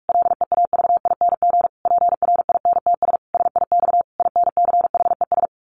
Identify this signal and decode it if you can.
Morse code, heard as a keyed tone, about 44 wpm, 700 Hz. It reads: LEAVING PRINTS SIX INCHES
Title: A Sound Of Thunder 45wpm